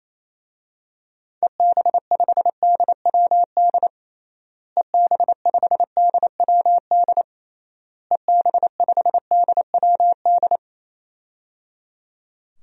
Morse code, heard as a keyed tone, about 28 wpm, 700 Hz. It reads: EB5DWD EB5DWD EB5DWD
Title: DXCC-100 Call Signs - 6 Characters - Repeated 3x - 28wpm